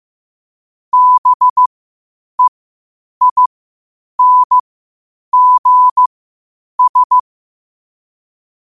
Morse code, heard as a keyed tone, about 15 wpm, spaced as for 8 wpm, 1000 Hz. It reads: BEINGS